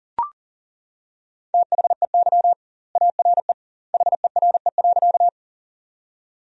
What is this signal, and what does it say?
Morse code, heard as a keyed tone, about 40 wpm, 700 Hz. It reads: THEY ARE HERE.
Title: 2x Word-Spacing - Sentences from Top 200 Words - 40wpm